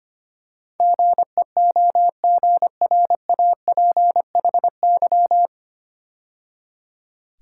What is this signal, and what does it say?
Morse code, heard as a keyed tone, about 25 wpm, 700 Hz. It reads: GEOGRAPHY
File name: practice / 25wpm / 9 Letter Words - First 3 Letters Given - 25wpm